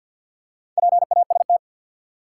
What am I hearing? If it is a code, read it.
Morse code, heard as a keyed tone, about 54 wpm, 700 Hz. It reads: PAST
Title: Four Letter Words - 50wpm